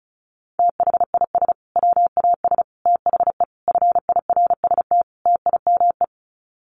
Morse code, heard as a keyed tone, about 34 wpm, 700 Hz. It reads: THIS WAS THE FIRST TIME
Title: Sets of 5 Words Encore 35wpm